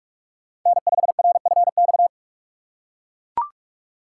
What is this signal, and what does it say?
Morse code, heard as a keyed tone, about 41 wpm, 700 Hz. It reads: N5RFX
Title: USA Call Signs - 5 Characters - 4x Character Spacing - 45wpm